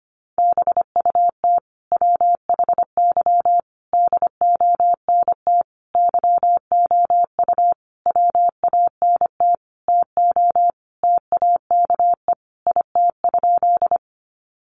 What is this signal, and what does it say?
Morse code, heard as a keyed tone, about 25 wpm, 700 Hz. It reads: BUT WHY DONT YOU WANT TO TAKE IT?
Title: Sentences from Top 300 Words 25wpm